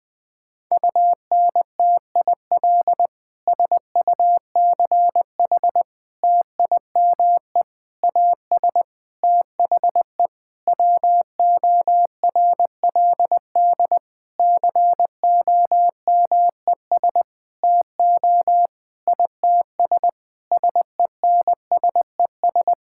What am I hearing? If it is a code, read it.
Morse code, heard as a keyed tone, about 21 wpm, 700 Hz. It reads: UNTIL SUCH TIME AS THE WORLD COMES TO ITS SENSES
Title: Sentences with 9-Word Phrases - Spoken Beginning - 20wpm